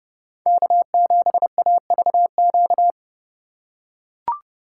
Morse code, heard as a keyed tone, about 32 wpm, 700 Hz. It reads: K7AVQ
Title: USA Call Signs - 5 Characters - 3x Character Spacing - 30wpm